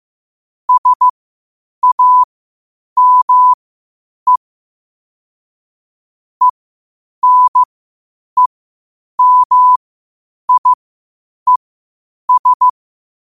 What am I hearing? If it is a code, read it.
Morse code, heard as a keyed tone, about 15 wpm, spaced as for 8 wpm, 1000 Hz. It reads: SAME ENEMIES
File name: 8wpm